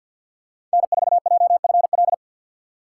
Morse code, heard as a keyed tone, about 48 wpm, 700 Hz. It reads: N4JFL
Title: USA Call Signs - 6x Character Spacing - 50wpm